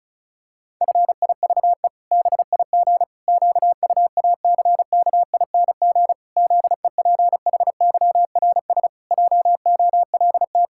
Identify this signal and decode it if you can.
Morse code, heard as a keyed tone, about 36 wpm, 700 Hz. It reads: FIVE BIG QUACKING ZEPHYRS JOLT
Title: Pangram Challenge v2 - 35wpm